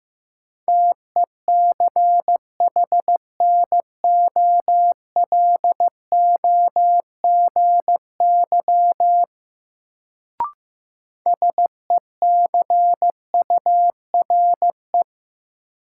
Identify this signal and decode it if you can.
Morse code, heard as a keyed tone, about 15 wpm, 700 Hz. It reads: TECHNOLOGY SECURE
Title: Sentences from Top 1200 Words - New Word Drill Alt - 15wpm